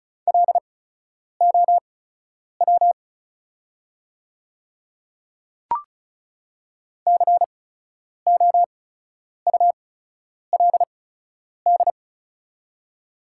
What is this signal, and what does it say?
Morse code, heard as a keyed tone, about 36 wpm, spaced as for 10 wpm, 690 Hz. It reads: LOW COULD
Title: 8x Character Spacing - Top 200 Words - 35wpm